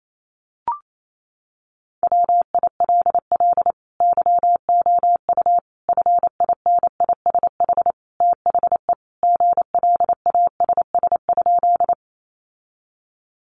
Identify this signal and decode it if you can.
Morse code, heard as a keyed tone, about 28 wpm, 700 Hz. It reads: WILL YOU FINISH THE GLASS?
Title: Sentences from Top 1300 Words 28wpm